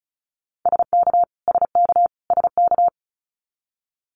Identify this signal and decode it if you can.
Morse code, heard as a keyed tone, about 34 wpm, 700 Hz. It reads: SK SK SK
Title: Canadian Provinces and Territories - Repeated 3x - 35wpm